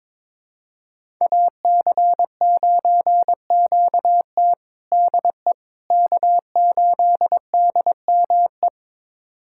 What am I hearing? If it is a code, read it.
Morse code, heard as a keyed tone, about 22 wpm, 700 Hz. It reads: AC9QT DE K8DME
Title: USA Callsign DE Callsign - 5 Characters - 22wpm